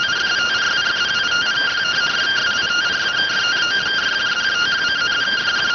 Two-tone teleprinter signal, about 170 Hz shift.